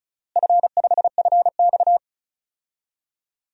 Morse code, heard as a keyed tone, about 36 wpm, 700 Hz. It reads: F5FX
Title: DXCC-100 Call Signs - 4 Characters - Repeated 3x - 35wpm